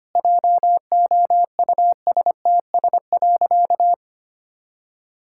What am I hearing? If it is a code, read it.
Morse code, heard as a keyed tone, about 25 wpm, 700 Hz. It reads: JOUSTS.
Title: Pangram Challenge v2 - 25wpm